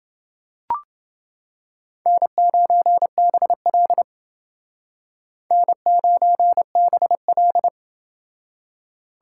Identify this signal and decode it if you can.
Morse code, heard as a keyed tone, about 28 wpm, 700 Hz. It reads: N9BL N9BL